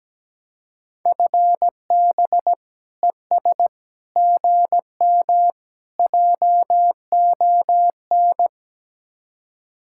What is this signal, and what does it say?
Morse code, heard as a keyed tone, about 17 wpm, 700 Hz. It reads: FB ES GM JON
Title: Speed-Racing Common QSO Phrases - 17wpm